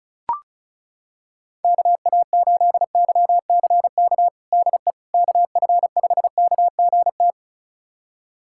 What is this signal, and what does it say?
Morse code, heard as a keyed tone, about 36 wpm, 700 Hz. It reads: KA8YCK DE KF5KGT
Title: USA Callsign DE Callsign - 6 Characters - 35wpm